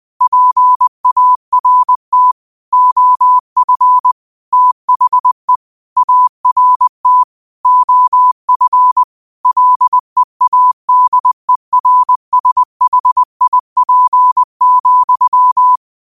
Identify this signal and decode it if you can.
Morse code, heard as a keyed tone, about 20 wpm, 1000 Hz. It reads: PART OF THE ART OF LEADERSHIP,
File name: qotdfiles / 20wpm